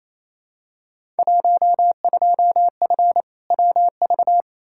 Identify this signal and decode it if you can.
Morse code, heard as a keyed tone, about 28 wpm, 700 Hz. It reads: 12F WV
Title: Field Day Exchanges Encore - 28wpm